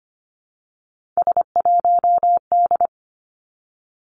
Morse code, heard as a keyed tone, about 25 wpm, 700 Hz. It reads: S1D
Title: Three Challenging Letter-Number Combinations 25wpm